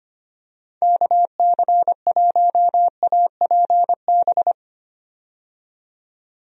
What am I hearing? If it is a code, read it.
Morse code, heard as a keyed tone, about 25 wpm, 700 Hz. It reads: KC1APB
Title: USA Call Signs - 6 Characters - 25wpm